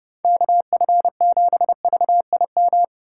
Morse code, heard as a keyed tone, about 30 wpm, 700 Hz. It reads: KF7VIM
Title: USA Call Signs - 6 Characters - 6x Character Spacing - 30wpm